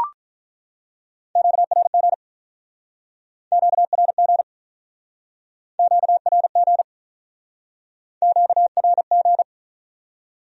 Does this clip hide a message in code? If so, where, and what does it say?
Yes: Morse code from 1.4 to 9.4 s, heard as a keyed tone, about 48 wpm, 700 Hz. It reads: QRG QRG QRG QRG